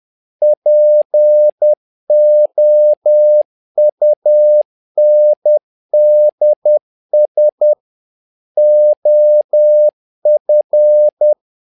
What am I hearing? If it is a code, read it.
Morse code, heard as a keyed tone, about 10 wpm, 600 Hz. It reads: POUNDS OF
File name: learning center / tower trouble / tower trouble 10wpm